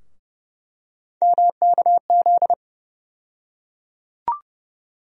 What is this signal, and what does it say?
Morse code, heard as a keyed tone, about 30 wpm, 700 Hz. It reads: MKZ
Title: USA Call Sign Postfixes - 3 Letters 30wpm